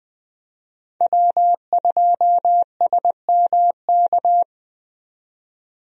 Morse code, heard as a keyed tone, about 20 wpm, 700 Hz. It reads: W2SMK
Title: USA Call Signs - 5 Characters - 20wpm